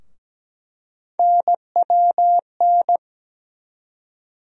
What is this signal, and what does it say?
Morse code, heard as a keyed tone, about 17 wpm, 700 Hz. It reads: NWN
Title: USA Call Sign Postfixes - 3 Letters 17wpm